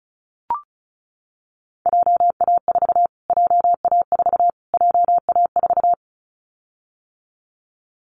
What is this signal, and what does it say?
Morse code, heard as a keyed tone, about 34 wpm, 700 Hz. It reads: JA4 JA4 JA4
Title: Callsign Prefixes for Top 25 DXCC - Repeated 3x - 35wpm